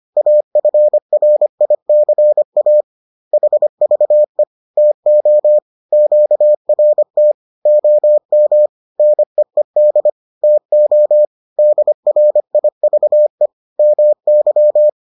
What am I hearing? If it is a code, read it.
Morse code, heard as a keyed tone, about 25 wpm, 600 Hz. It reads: AFRICA HVE TO QRT OM NEED TO DRIVE MY
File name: 25wpm-6